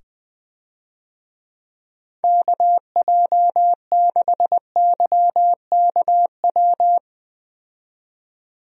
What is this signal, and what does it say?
Morse code, heard as a keyed tone, about 20 wpm, 700 Hz. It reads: KJ6YKW
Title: USA Call Signs - 6 Characters - 20wpm